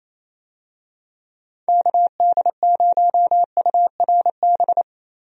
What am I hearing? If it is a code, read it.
Morse code, heard as a keyed tone, about 28 wpm, 700 Hz. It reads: KD0URB
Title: USA Call Signs - 6 Characters - Repeated 3x - 28wpm